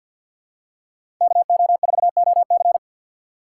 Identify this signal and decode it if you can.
Morse code, heard as a keyed tone, about 50 wpm, 700 Hz. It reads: KO4OC
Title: USA Call Signs - 5 Characters - 3x Character Spacing - 50wpm